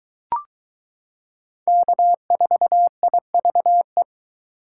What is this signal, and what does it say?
Morse code, heard as a keyed tone, about 22 wpm, 700 Hz. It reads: K4IVE